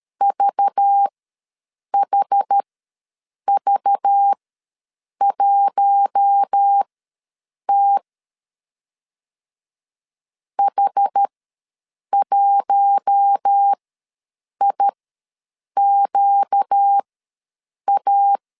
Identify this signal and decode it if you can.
Morse code, heard as a keyed tone, about 13 wpm, spaced as for 7 wpm, 800 Hz. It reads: VHV1T H1IQA
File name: LEZIONI / 5 LBVC / 5 LBV ws 12wpm 3015